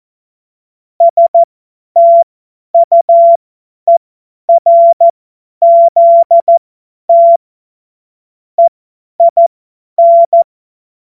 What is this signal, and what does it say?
Morse code, heard as a keyed tone, about 14 wpm, spaced as for 10 wpm, 680 Hz. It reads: STUERZT EIN
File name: Witz-02-September-2025-50-F70